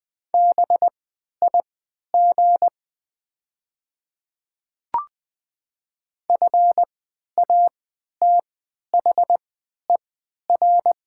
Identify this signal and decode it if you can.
Morse code, heard as a keyed tone, about 20 wpm, spaced as for 11 wpm, 700 Hz. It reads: BIG FATHER
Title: 3x Character Spacing - Top 200 Words - 20wpm